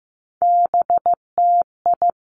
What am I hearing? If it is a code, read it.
Morse code, heard as a keyed tone, about 15 wpm, 700 Hz. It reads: BTI
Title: Three Challenging Letter-Number Combinations 15wpm